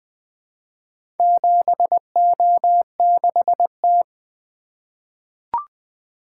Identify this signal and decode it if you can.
Morse code, heard as a keyed tone, about 20 wpm, 700 Hz. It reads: 7O6T
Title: Four Letter-Number Combinations - 20wpm